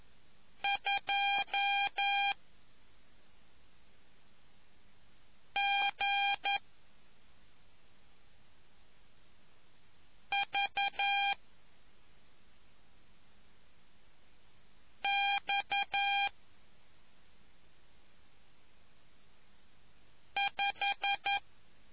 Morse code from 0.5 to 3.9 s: 2